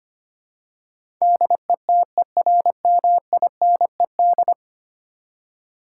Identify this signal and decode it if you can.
Morse code, heard as a keyed tone, about 25 wpm, 700 Hz. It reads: DETERMINED
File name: 10 Letter Words - First 3 Letters Given - 25wpm